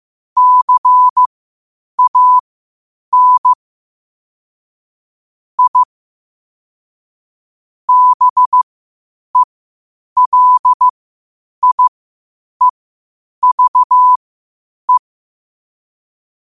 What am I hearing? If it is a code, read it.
Morse code, heard as a keyed tone, about 15 wpm, spaced as for 8 wpm, 1000 Hz. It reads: CAN I BELIEVE
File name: qotdfiles / 8wpm